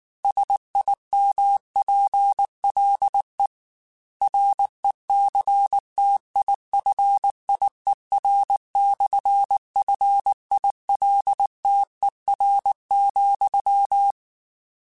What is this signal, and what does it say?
Morse code, heard as a keyed tone, about 19 wpm, 800 Hz. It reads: SIMPLE RECTIFIER/FILTER,
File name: Technical Subjects / Technical Subjects-mp3s / Sw Pwr Sup @18